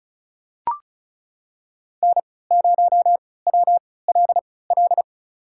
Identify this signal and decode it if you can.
Morse code, heard as a keyed tone, about 36 wpm, spaced as for 20 wpm, 700 Hz. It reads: N0WLL
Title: USA Call Signs - 5 Characters - 3x Character Spacing - 35wpm